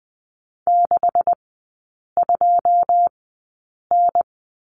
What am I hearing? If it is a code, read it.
Morse code, heard as a keyed tone, about 20 wpm, spaced as for 8 wpm, 700 Hz. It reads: 62N
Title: 152 Lesson 25e - Letter p - Three Characters 20wpm